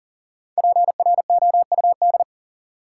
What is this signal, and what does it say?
Morse code, heard as a keyed tone, about 40 wpm, 700 Hz. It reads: PROUD
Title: Five Letter Words - 40wpm